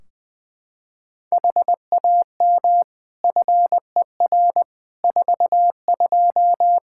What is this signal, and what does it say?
Morse code, heard as a keyed tone, about 21 wpm, 700 Hz. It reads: HAM FER 42
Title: Practice QSO - Exchange 3 - More Info - 20wpm